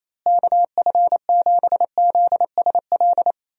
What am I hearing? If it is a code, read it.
Morse code, heard as a keyed tone, about 27 wpm, 700 Hz. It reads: KF7ZSL
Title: USA Call Signs - 6 Characters - Repeated 2x - 28wpm